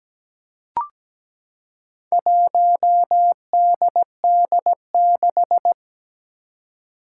Morse code, heard as a keyed tone, about 17 wpm, 700 Hz. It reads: 1DD6
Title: Four Challenging Letter-Number Combinations 17wpm